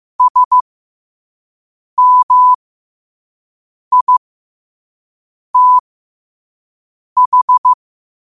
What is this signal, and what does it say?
Morse code, heard as a keyed tone, about 15 wpm, spaced as for 5 wpm, 1000 Hz. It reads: SMITH
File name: 5wpm